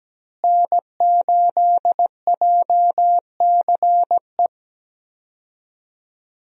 Morse code, heard as a keyed tone, about 17 wpm, 700 Hz. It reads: N8JCE
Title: USA Call Signs - 5 Characters - Repeated 3x - 17wpm